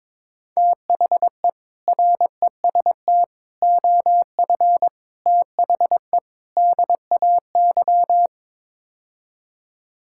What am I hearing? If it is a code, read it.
Morse code, heard as a keyed tone, about 22 wpm, 700 Hz. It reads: THE REST OF THE DAY
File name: practice / 22wpm / Build A Phrase - 5-Word Phrases - 22wpm